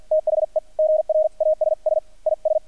Morse code, heard as a keyed tone, about 50 wpm, spaced as for 43 wpm, 620 Hz. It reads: THE GRASS IS
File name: archived / CRSnet